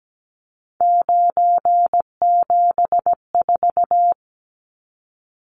Morse code, heard as a keyed tone, about 17 wpm, 700 Hz. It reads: 974
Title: Numbers - 3-Digit 17wpm